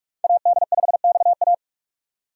Morse code, heard as a keyed tone, about 44 wpm, 700 Hz. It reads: AD5XA